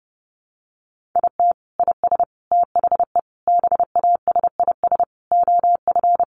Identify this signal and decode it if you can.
Morse code, heard as a keyed tone, about 30 wpm, 700 Hz. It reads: IT IS THE BASIS OF
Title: Sentences from Top 1200 Words 30wpm